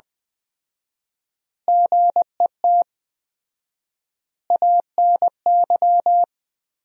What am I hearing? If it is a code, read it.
Morse code, heard as a keyed tone, about 21 wpm, 700 Hz. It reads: GET ANY
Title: 4x Word-Spacing - Sentences from Top 400 Words - 20wpm